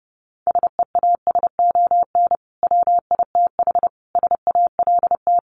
Morse code, heard as a keyed tone, about 30 wpm, 700 Hz. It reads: SEASON WITH SALT
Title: Sets of 5 Words 30wpm